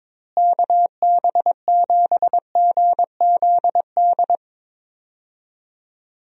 Morse code, heard as a keyed tone, about 22 wpm, 700 Hz. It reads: KB7GZD
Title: USA Call Signs - 6 Characters - Repeated 2x - 22wpm